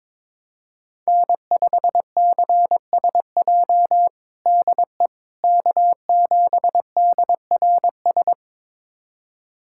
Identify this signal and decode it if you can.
Morse code, heard as a keyed tone, about 22 wpm, 700 Hz. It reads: N5CSJ DE K7DRS
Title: USA Callsign DE Callsign - 5 Characters - 22wpm